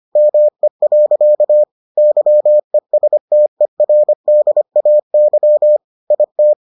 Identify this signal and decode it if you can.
Morse code, heard as a keyed tone, about 25 wpm, 600 Hz. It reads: ME. YESTERDAY IT